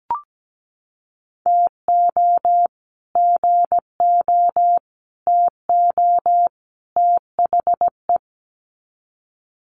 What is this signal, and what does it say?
Morse code, heard as a keyed tone, about 17 wpm, 700 Hz. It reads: TO GO TO THE
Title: Sets of 4 Words 17wpm